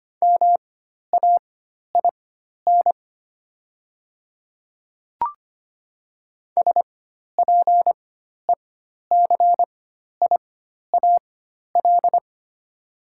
Morse code, heard as a keyed tone, about 25 wpm, spaced as for 12 wpm, 690 Hz. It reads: MAIN SPECIAL
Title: 4x Character Spacing - Top 500 Words - 25wpm